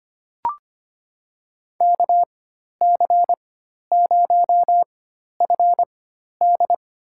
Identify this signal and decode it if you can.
Morse code, heard as a keyed tone, about 25 wpm, spaced as for 12 wpm, 690 Hz. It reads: KC0FD
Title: USA Call Signs - 4x Character Spacing - 25wpm